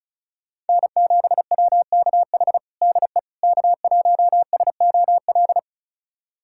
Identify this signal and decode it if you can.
Morse code, heard as a keyed tone, about 36 wpm, 700 Hz. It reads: N7WKH DE K1SOL
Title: USA Callsign DE Callsign - 5 Characters - 35wpm